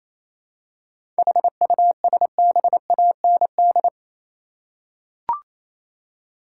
Morse code, heard as a keyed tone, about 28 wpm, 700 Hz. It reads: HUSBAND
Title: Seven Letter Words - 28wpm